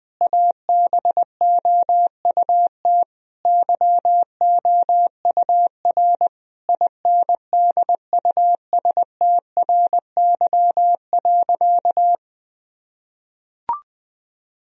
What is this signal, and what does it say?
Morse code, heard as a keyed tone, about 20 wpm, 700 Hz. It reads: ABOUT YOUR INDUSTRY.
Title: Sentences from Top 1000 Words - Review Part 4 - 20wpm